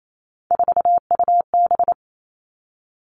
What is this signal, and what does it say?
Morse code, heard as a keyed tone, about 28 wpm, 700 Hz. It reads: 4UB
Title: Three Challenging Letter-Number Combinations 28wpm